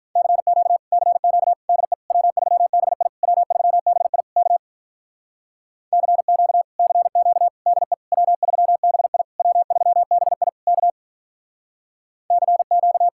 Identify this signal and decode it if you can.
Morse code, heard as a keyed tone, about 44 wpm, 700 Hz. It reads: CQ CQ DE W3BI W3BI K CQ CQ DE W3BI W3BI K CQ
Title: Speed-Racing Common QSO Phrases - 35wpm